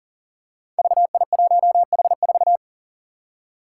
Morse code, heard as a keyed tone, about 40 wpm, 700 Hz. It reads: VI1H4
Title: Five Challenging Letter-Number Combinations 40wpm